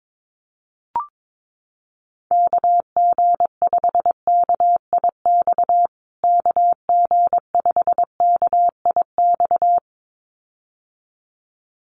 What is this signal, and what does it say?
Morse code, heard as a keyed tone, about 22 wpm, 700 Hz. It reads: KG5KIX KG5KIX
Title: USA Call Signs Repeated 2x 22wpm